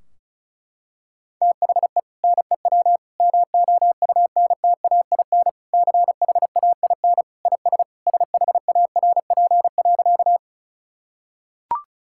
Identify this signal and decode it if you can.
Morse code, heard as a keyed tone, about 36 wpm, 700 Hz. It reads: THE NEW MOUNTAIN CHAIN IS SHARP.
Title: Sentences from Top 1600 Words 35wpm